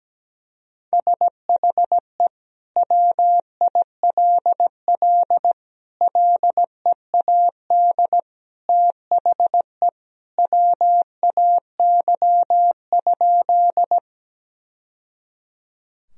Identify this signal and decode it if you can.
Morse code, heard as a keyed tone, about 17 wpm, 700 Hz. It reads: SHE WILL LEAD THE WAY?
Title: Sentences from Top 1000 Words - Review Part 2 - 17wpm